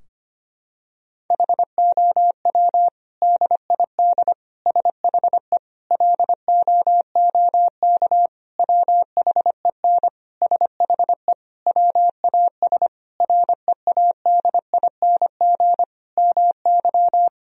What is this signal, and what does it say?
Morse code, heard as a keyed tone, about 25 wpm, 700 Hz. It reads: HOW DID SHE LOOK WHEN SHE WAS READING MY
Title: Sentences from Top 1000 Words - Review Part 3 - 25wpm